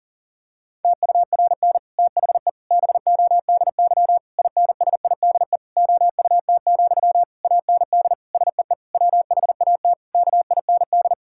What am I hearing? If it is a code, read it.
Morse code, heard as a keyed tone, about 40 wpm, 700 Hz. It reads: TURN THE BODY INSIDE OUT, AND SEE WHAT KIND
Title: Sentences from Top 1000 Words - Review Part 6 - 40wpm